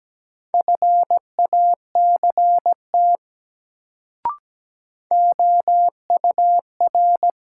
Morse code, heard as a keyed tone, about 17 wpm, 700 Hz. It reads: FACT OUR
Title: Sets of 2 Words - New Word Drill - 17wpm